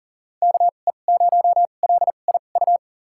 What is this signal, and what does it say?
Morse code, heard as a keyed tone, about 40 wpm, spaced as for 29 wpm, 700 Hz. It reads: KE0LIU
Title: USA Call Signs - 6 Characters - 2x Character Spacing - 40wpm